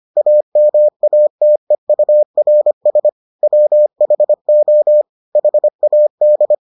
Morse code, heard as a keyed tone, about 25 wpm, 600 Hz. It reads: AMATEURS WHO HAD